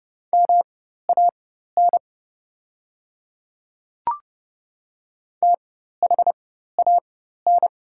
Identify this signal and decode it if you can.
Morse code, heard as a keyed tone, about 30 wpm, spaced as for 14 wpm, 700 Hz. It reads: MAN THAN
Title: 4x Character Spacing - Top 200 Words - 30wpm